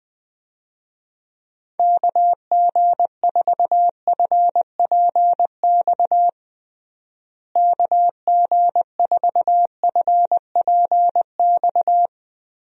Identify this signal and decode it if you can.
Morse code, heard as a keyed tone, about 20 wpm, 700 Hz. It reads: KG4FPX KG4FPX
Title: USA Call Signs - 6 Characters - Repeated 3x - 20wpm